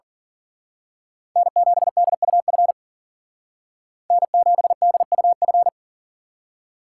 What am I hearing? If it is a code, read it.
Morse code, heard as a keyed tone, about 43 wpm, 700 Hz. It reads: N7DUF N7DUF
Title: Speed Racing - USA 1x3 Call Signs 40wpm